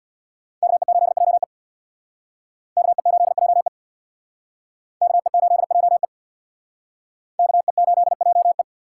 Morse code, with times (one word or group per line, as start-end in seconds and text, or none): 0.6-1.4 KE8JE
2.8-3.7 KE8JE
5.0-6.0 KE8JE
7.4-8.6 KE8JE